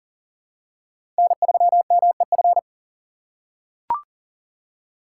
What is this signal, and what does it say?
Morse code, heard as a keyed tone, about 40 wpm, 700 Hz. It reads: N3MEF
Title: USA Call Signs - 5 Characters - 4x Character Spacing - 40wpm